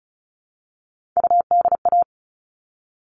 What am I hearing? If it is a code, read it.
Morse code, heard as a keyed tone, about 34 wpm, 700 Hz. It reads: UDA
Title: All Cut Numbers - 3-Digit 35wpm